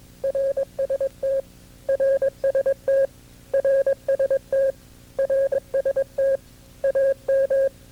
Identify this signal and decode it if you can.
Morse code, heard as a keyed tone, about 24 wpm, 540 Hz. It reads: RST RST RST RST AM
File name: CW-practice/KB6MT-tape2-side1